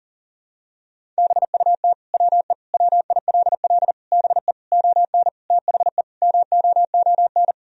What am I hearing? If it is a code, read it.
Morse code, heard as a keyed tone, about 40 wpm, 700 Hz. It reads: BUT WE WILL BE ON THE MOON
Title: Sentences from Top 1000 Words - Review Part 3 - 40wpm